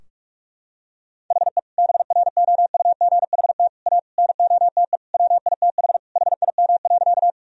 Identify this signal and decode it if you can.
Morse code, heard as a keyed tone, about 46 wpm, 700 Hz. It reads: HE BROUGHT A NOTE WITH HIM.
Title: Sentences from Top 1000 Words - Review Part 2 - 45wpm